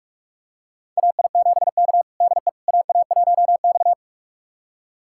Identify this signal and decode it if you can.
Morse code, heard as a keyed tone, about 46 wpm, 700 Hz. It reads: AI7K DE AA1X
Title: USA Callsign DE Callsign - 4 Characters - 45wpm